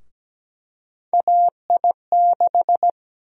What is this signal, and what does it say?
Morse code, heard as a keyed tone, about 17 wpm, 700 Hz. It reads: AI6
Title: USA Call Sign Prefixes - 17wpm